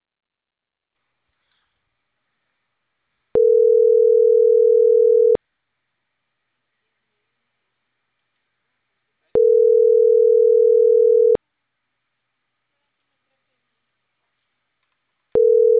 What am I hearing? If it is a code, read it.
Ringback tone (North America), 440 + 480 Hz.